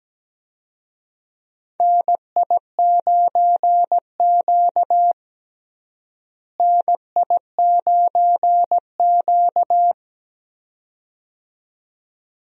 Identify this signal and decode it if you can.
Morse code, heard as a keyed tone, about 17 wpm, 700 Hz. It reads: NI9Q NI9Q
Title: USA Call Signs - 4 Characters - Repeated 2x - 17wpm